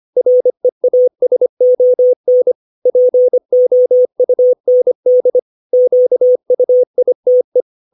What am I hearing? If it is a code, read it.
Morse code, heard as a keyed tone, about 25 wpm, 500 Hz. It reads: REASON POUND QUITE
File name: 632 words / 632 Words 25WPM